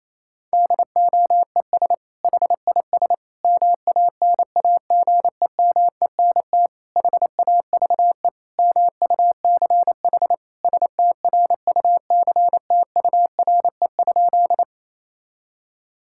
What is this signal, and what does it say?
Morse code, heard as a keyed tone, about 28 wpm, 700 Hz. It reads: DOES HIS MANAGEMENT HAVE MUCH STRUCTURE?
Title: Sentences from Top 1000 Words - Review Part 6 - 28wpm